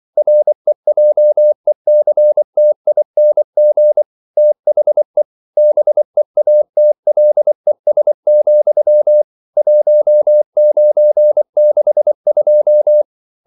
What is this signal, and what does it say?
Morse code, heard as a keyed tone, about 24 wpm, 600 Hz. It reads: REJECTING THE BEATLES, 1962
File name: CW / Practice / EnglishQuotations0424